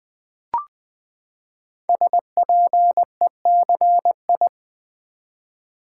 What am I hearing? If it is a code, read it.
Morse code, heard as a keyed tone, about 21 wpm, 700 Hz. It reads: SPECI
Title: Build A Word - 5 Letters At A Time - 12-Letter Words - 20wpm